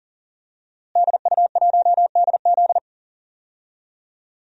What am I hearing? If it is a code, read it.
Morse code, heard as a keyed tone, about 40 wpm, 700 Hz. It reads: DU1DZ
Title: DXCC-100 Call Signs - 5 Characters - Repeated 3x - 40wpm